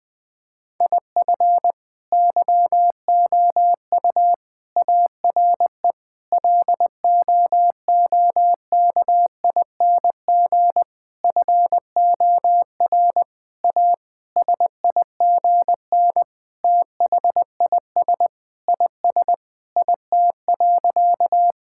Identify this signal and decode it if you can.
Morse code, heard as a keyed tone, about 20 wpm, 700 Hz. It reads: IF YOU ARE LOOKING FOR A SIGN THIS IS IT.
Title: Sentences with 6-Word Phrases - Spoken Beginning - 20wpm